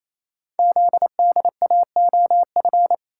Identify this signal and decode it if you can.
Morse code, heard as a keyed tone, about 28 wpm, 700 Hz. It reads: ZDAOF
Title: Code Groups - 5 Letter:Numbers - 28wpm